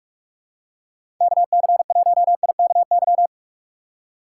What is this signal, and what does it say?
Morse code, heard as a keyed tone, about 45 wpm, 700 Hz. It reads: KC1IKY
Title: USA Call Signs - 6 Characters - 6x Character Spacing - 45wpm